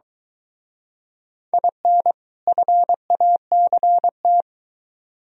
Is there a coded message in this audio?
Yes: Morse code from 1.5 to 4.4 s, heard as a keyed tone, about 23 wpm, 700 Hz. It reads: IN FACT